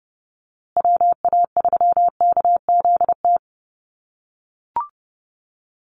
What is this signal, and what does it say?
Morse code, heard as a keyed tone, about 30 wpm, 700 Hz. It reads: WA3KZT
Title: USA Call Signs Repeated 3x 30wpm